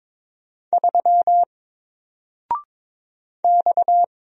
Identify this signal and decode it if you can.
Morse code, heard as a keyed tone, about 22 wpm, 700 Hz. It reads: 3 X
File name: Single-Letter-Number-22wpm